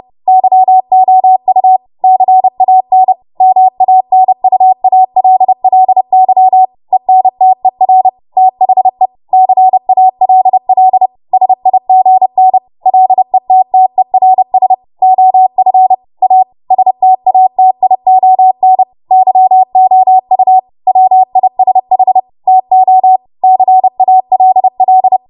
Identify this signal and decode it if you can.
Morse code, heard as a keyed tone, about 30 wpm, 750 Hz. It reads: YOU CAN MANUALLY ENTER THE CALL SIGN LETTERS OF A STATION YOU WISH TO CALL
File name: Morse / 30WPM